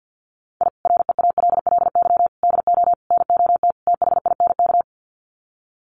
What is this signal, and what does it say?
Morse code, heard as a keyed tone, about 50 wpm, 700 Hz. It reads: I REALLY DO NOT THINK
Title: Sets of 5 Words Encore2 50wpm